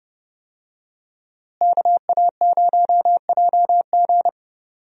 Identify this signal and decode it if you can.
Morse code, heard as a keyed tone, about 31 wpm, 700 Hz. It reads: KA0JG
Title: USA Call Signs - 5 Characters - Repeated 3x - 30wpm